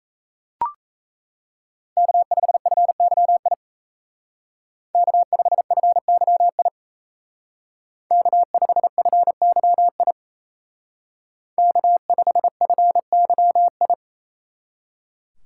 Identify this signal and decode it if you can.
Morse code, heard as a keyed tone, about 36 wpm, 700 Hz. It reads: K5FYI K5FYI K5FYI K5FYI